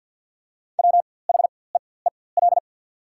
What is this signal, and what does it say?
Morse code, heard as a keyed tone, about 54 wpm, spaced as for 24 wpm, 700 Hz. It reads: WHEEL
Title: 4x Character Spacing - Top 500 Words - 50wpm